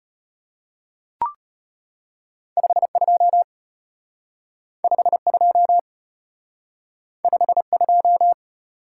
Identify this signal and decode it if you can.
Morse code, heard as a keyed tone, about 32 wpm, 700 Hz. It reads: 52 52 52